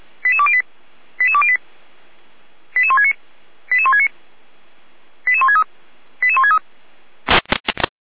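Five-tone selective call (ZVEI), heard as quick runs of five tones, tones of about 70 ms. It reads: 88208 88208 88169 88169 88163 88163
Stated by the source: ZVEI five-tone selective call received via SDR.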